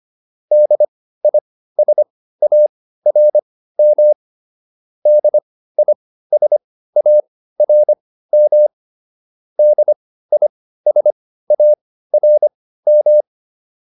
Morse code, heard as a keyed tone, about 25 wpm, spaced as for 15 wpm, 600 Hz. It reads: DISARM DISARM DISARM
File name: DIS6 15